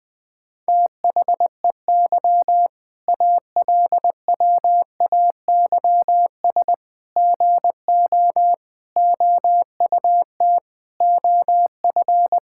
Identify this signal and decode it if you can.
Morse code, heard as a keyed tone, about 20 wpm, 700 Hz. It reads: THEY ALWAYS GO OUT OF
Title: Sentences with 6-Word Phrases - 20wpm